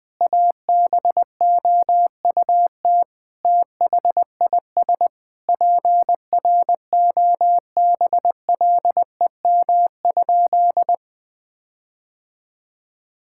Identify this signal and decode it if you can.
Morse code, heard as a keyed tone, about 21 wpm, 700 Hz. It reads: ABOUT THIS PROBLEM?
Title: Sentences with 8-Word Phrases - 20wpm